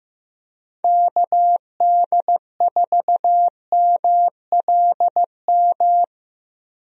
Morse code, heard as a keyed tone, about 16 wpm, 700 Hz. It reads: KD4MLM
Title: USA Call Signs - 6 Characters - Repeated 2x - 15wpm